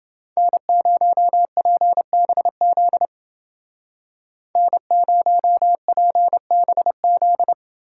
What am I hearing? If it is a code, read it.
Morse code, heard as a keyed tone, about 29 wpm, 700 Hz. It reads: N0PBZ N0PBZ